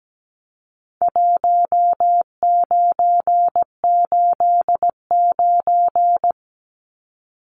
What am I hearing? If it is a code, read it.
Morse code, heard as a keyed tone, about 17 wpm, 700 Hz. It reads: 1989
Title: Numbers - 4-Digit 17wpm